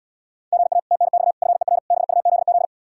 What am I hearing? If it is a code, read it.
Morse code, heard as a keyed tone, about 77 wpm, 700 Hz. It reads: GA TNX FER BUZZ